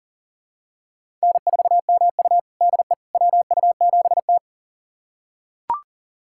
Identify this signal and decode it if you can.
Morse code, heard as a keyed tone, about 40 wpm, 700 Hz. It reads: N4MU DE WU7T
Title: USA Callsign DE Callsign - 4 Characters - 40wpm